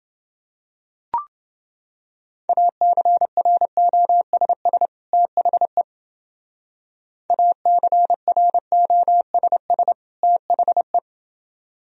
Morse code, heard as a keyed tone, about 29 wpm, 700 Hz. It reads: ACROSS THE ACROSS THE